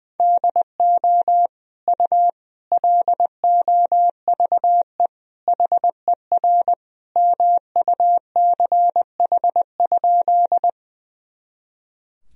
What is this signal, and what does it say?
Morse code, heard as a keyed tone, about 20 wpm, 700 Hz. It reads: DO U LOVE HER MUCH?
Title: Sentences from Top 400 Words and Abbreviations 20wpm